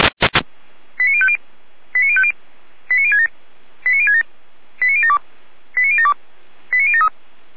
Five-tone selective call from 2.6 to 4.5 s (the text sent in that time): 89067 89067